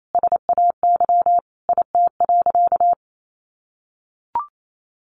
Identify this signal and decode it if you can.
Morse code, heard as a keyed tone, about 28 wpm, 700 Hz. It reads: SAY IT.
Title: Sentences from Top 1000 Words 28wpm